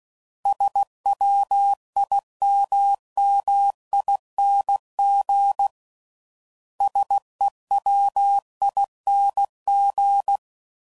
Morse code, heard as a keyed tone, about 16 wpm, 800 Hz. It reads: SWIMMING SEWING
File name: Suffixes / Special INGs / Sngl Ings